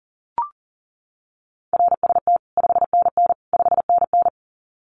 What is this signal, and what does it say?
Morse code, heard as a keyed tone, about 40 wpm, 700 Hz. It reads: RST 5NN 5NN
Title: Common QSO Phrases Encore 40wpm